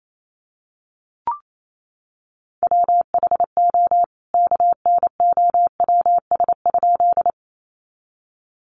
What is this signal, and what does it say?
Morse code, heard as a keyed tone, about 28 wpm, 700 Hz. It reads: WHO KNOWS?
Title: A Sound Of Thunder 28wpm